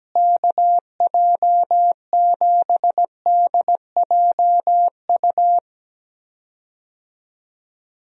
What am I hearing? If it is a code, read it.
Morse code, heard as a keyed tone, about 17 wpm, 700 Hz. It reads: KJ7DJU
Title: USA Call Signs - 6 Characters - Repeated 3x - 17wpm